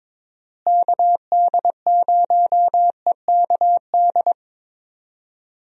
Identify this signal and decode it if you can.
Morse code, heard as a keyed tone, about 22 wpm, 700 Hz. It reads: KD0EKD
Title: USA Call Signs - 6 Characters - Repeated 3x - 22wpm